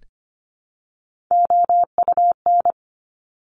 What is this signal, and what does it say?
Morse code, heard as a keyed tone, about 25 wpm, 700 Hz. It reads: OUN
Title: Top Three Letter Combinations in Words 25wpm